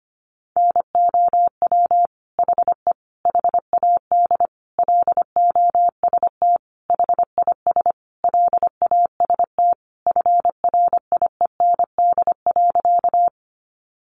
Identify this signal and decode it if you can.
Morse code, heard as a keyed tone, about 25 wpm, 700 Hz. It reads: NOW HE HAD LOST HIS LAST FRIEND.
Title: Sentences from Top 1400 Words 25wpm